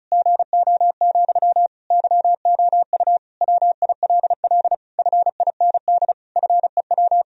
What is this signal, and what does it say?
Morse code, heard as a keyed tone, about 36 wpm, 700 Hz. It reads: GO, YOU WILL FIND FEW
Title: Sentences from Top 1000 Words - Review Part 1 - 35wpm